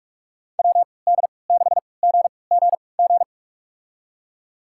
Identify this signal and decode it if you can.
Morse code, heard as a keyed tone, about 44 wpm, spaced as for 25 wpm, 700 Hz. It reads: WD6GGG
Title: USA Call Signs - 6 Characters - 3x Character Spacing - 45wpm